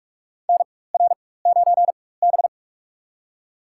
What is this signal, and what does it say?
Morse code, heard as a keyed tone, about 46 wpm, spaced as for 21 wpm, 700 Hz. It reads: NR9B
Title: USA Call Signs - 4 Characters - 4x Character Spacing - 45wpm